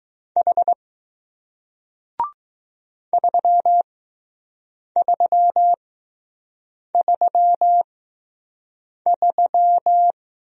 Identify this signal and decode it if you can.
Morse code, heard as a keyed tone, about 20 wpm, spaced as for 6 wpm, 700 Hz. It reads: H 3333